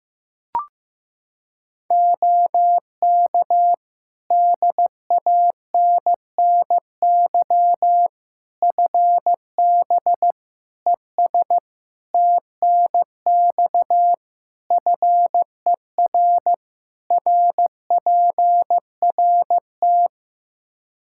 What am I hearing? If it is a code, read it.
Morse code, heard as a keyed tone, about 15 wpm, 700 Hz. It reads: OK DANNY FB ES TNX FER RPRT
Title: Practice QSO - Exchange 2 - Info - 15wpm